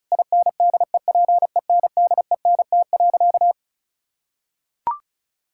Morse code, heard as a keyed tone, about 36 wpm, 700 Hz. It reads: INDEPENDENT.
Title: Sentences from Top 1000 Words - Review Part 6 - 35wpm